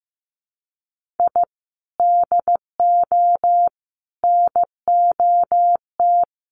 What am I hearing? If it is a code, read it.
Morse code, heard as a keyed tone, about 15 wpm, 700 Hz. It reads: I DO NOT
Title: Sets of 5 Words Encore2 15wpm